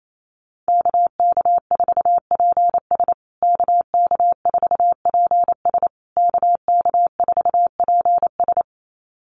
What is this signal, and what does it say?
Morse code, heard as a keyed tone, about 28 wpm, 700 Hz. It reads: KK4PS KK4PS KK4PS
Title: USA Call Signs Repeated 3x 28wpm